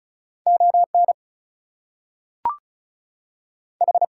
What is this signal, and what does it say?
Morse code, heard as a keyed tone, about 36 wpm, 700 Hz. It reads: ON H
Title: Two Letter Words - 35wpm